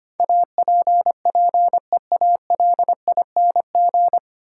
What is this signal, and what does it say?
Morse code, heard as a keyed tone, about 25 wpm, 690 Hz. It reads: APPEALING
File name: practice / 25wpm / 9 Letter Words - First 3 Letters Given - 25wpm